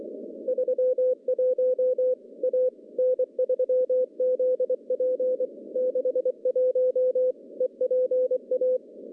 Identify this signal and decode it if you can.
Morse code, heard as a keyed tone, about 23 wpm, 510 Hz. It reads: 31 A N3ZP 61 EPA
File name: contests / CW-SS-k5tr- -1200